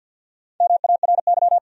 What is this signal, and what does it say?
Morse code, heard as a keyed tone, about 52 wpm, 700 Hz. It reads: MARY
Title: USA Common Names - Max 4 Letters - 50wpm